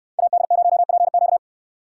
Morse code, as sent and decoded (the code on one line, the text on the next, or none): .- -.. ----- .--. --.-
AD0PQ